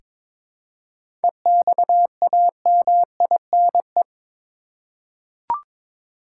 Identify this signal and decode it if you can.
Morse code, heard as a keyed tone, about 22 wpm, 700 Hz. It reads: EXAMINE
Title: Seven Letter Words - 22wpm